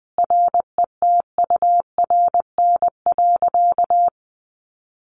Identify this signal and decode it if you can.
Morse code, heard as a keyed tone, about 20 wpm, 700 Hz. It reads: RETURN.
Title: Sentences from Top 800 Words 20wpm